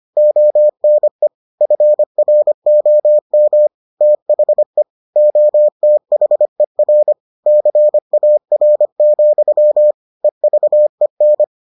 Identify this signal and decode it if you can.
Morse code, heard as a keyed tone, about 25 wpm, 600 Hz. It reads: ONE FROM THE OTHER CAR, EVEN